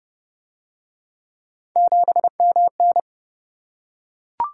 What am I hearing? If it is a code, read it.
Morse code, heard as a keyed tone, about 28 wpm, 700 Hz. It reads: 7MN